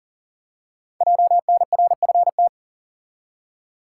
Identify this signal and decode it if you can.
Morse code, heard as a keyed tone, about 38 wpm, 700 Hz. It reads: JNRFT
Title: Code Groups - 5 Letter:Numbers - 40wpm